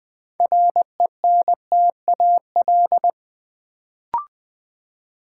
Morse code, heard as a keyed tone, about 20 wpm, 700 Hz. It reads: RENTAL
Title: Build A Word - 2 Letters At A Time - 6-Letter Words - 20wpm